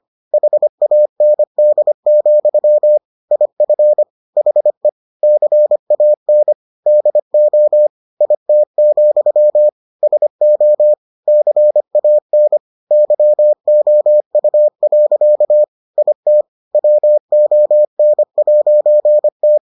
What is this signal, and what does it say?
Morse code, heard as a keyed tone, about 25 wpm, 600 Hz. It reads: HAND, IF HE CAN DO IT, SO CAN YOU. IT WON'T